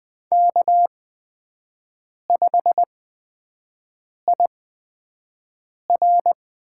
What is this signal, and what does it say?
Morse code, heard as a keyed tone, about 20 wpm, spaced as for 5 wpm, 700 Hz. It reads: K5IR
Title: USA Call Signs - 4 Characters - 8x Character Spacing - 20wpm